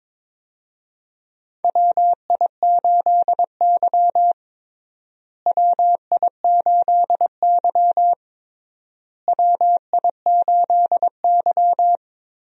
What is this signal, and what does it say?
Morse code, heard as a keyed tone, about 21 wpm, 700 Hz. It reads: WI8Y WI8Y WI8Y
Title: USA Call Signs - 4 Characters - Repeated 3x - 22wpm